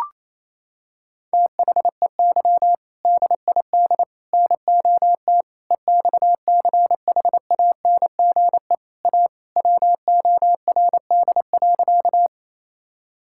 Morse code, heard as a keyed tone, about 28 wpm, 700 Hz. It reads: THEY DID NOT EXCHANGE A WORD.
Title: Sentences from Top 1000 Words - Review Part 6 - 28wpm